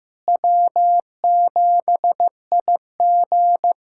Morse code, heard as a keyed tone, about 16 wpm, 700 Hz. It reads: W7IG
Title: USA Callsign DE Callsign - 4 Characters - 15wpm